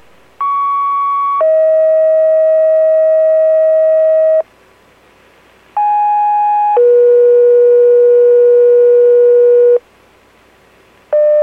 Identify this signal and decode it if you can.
Two-tone page, heard as a one-second tone, then a three-second tone, about 1120 then 640 Hz, about 830 then 480 Hz. It is heard from 0.4 to 9.8 s.